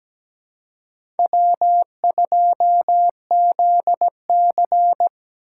Morse code, heard as a keyed tone, about 17 wpm, 700 Hz. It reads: W2ZC
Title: USA Call Signs - 4 Characters - 6x Character Spacing - 17wpm